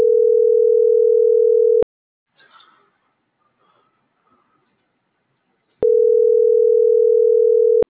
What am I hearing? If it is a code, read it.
Ringback tone (North America), 440 + 480 Hz.